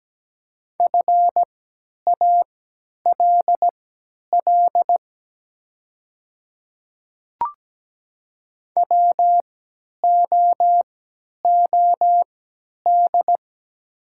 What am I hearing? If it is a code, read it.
Morse code, heard as a keyed tone, about 17 wpm, spaced as for 10 wpm, 700 Hz. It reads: FALL WOOD
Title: 3x Character Spacing - Top 500 Words - 17wpm